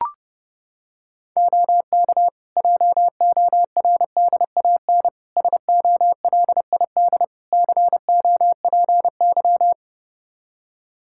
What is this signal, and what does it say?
Morse code, heard as a keyed tone, about 30 wpm, 700 Hz. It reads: OK JORDAN SOLID COPY
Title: Practice QSO - Exchange 3 - More Info - 30wpm